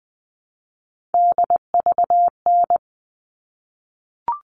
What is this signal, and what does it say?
Morse code, heard as a keyed tone, about 20 wpm, 700 Hz. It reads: DVN
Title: All Cut Numbers - 3-Digit 20wpm